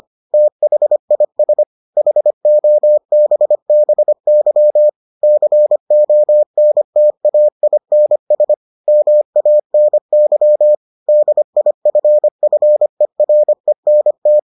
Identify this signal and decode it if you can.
Morse code, heard as a keyed tone, about 25 wpm, 600 Hz. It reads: THIS HOBBY CONTAINS MANY DIFFERENT